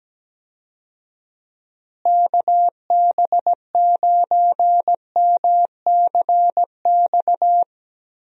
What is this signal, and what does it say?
Morse code, heard as a keyed tone, about 17 wpm, 700 Hz. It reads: KB9MCX
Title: USA Call Signs - 6 Characters - Repeated 3x - 17wpm